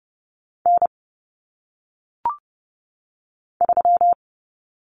Morse code, heard as a keyed tone, about 30 wpm, 700 Hz. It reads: N 3
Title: Common Cut Numbers - 1-Digit 30wpm